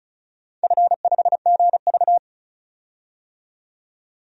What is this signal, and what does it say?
Morse code, heard as a keyed tone, about 34 wpm, 700 Hz. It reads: F5GV
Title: DXCC-100 Call Signs - 4 Characters - 35wpm